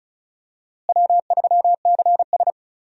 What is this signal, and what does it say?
Morse code, heard as a keyed tone, about 36 wpm, 700 Hz. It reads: W3CS
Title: USA Call Signs - 4 Characters - Repeated 2x - 35wpm